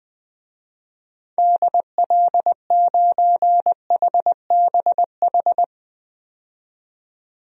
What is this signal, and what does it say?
Morse code, heard as a keyed tone, about 20 wpm, 700 Hz. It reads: DL9HBH
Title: DXCC-100 Call Signs - 6 Characters - 20wpm